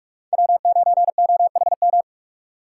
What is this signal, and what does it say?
Morse code, heard as a keyed tone, about 60 wpm, spaced as for 52 wpm, 700 Hz. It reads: W9OHM
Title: Speed Racing - USA 1x3 Call Signs 45wpm